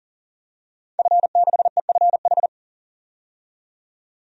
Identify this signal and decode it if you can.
Morse code, heard as a keyed tone, about 40 wpm, 700 Hz. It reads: F6EFH
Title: DXCC-100 Call Signs - 5 Characters - 40wpm